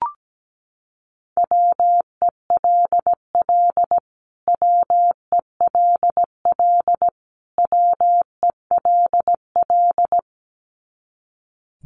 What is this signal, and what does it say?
Morse code, heard as a keyed tone, about 17 wpm, 700 Hz. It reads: WELL WELL WELL
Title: Top 300 Words Repeated 17wpm